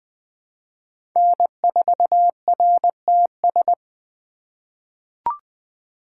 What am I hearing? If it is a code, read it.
Morse code, heard as a keyed tone, about 21 wpm, 700 Hz. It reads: N4RTS
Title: USA Call Signs - 6x Character Spacing - 20wpm